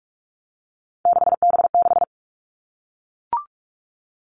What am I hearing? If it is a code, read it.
Morse code, heard as a keyed tone, about 44 wpm, 700 Hz. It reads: 6B6
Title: All Cut Numbers - 3-Digit 45wpm